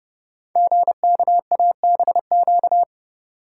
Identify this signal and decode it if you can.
Morse code, heard as a keyed tone, about 28 wpm, 700 Hz. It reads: GKABQ
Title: Code Groups - 5 Letters - 30wpm